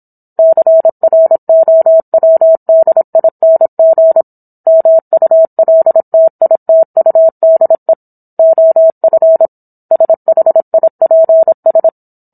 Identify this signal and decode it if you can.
Morse code, heard as a keyed tone, about 27 wpm, 650 Hz. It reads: CROWDING MULTITUDE OF SHIPS